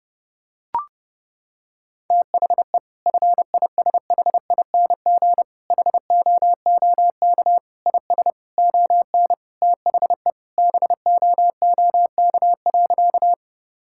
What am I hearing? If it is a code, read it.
Morse code, heard as a keyed tone, about 30 wpm, 700 Hz. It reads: THE FISHING HOOK IS ON THE BOOK.
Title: Sentences from Top 1600 Words 30wpm